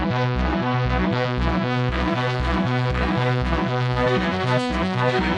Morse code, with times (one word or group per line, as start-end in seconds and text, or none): none